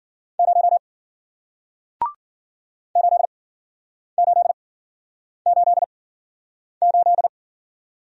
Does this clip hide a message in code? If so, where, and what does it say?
Yes: Morse code from 0.4 to 7.3 s, heard as a keyed tone, about 53 wpm, spaced as for 9 wpm, 700 Hz. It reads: 0 8888